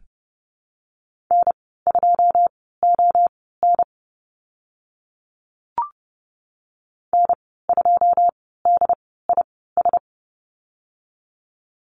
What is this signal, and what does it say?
Morse code, heard as a keyed tone, about 30 wpm, spaced as for 17 wpm, 700 Hz. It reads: N2ON N2DIS
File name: Lesson 15g - Number 5 - Call Signs 30wpm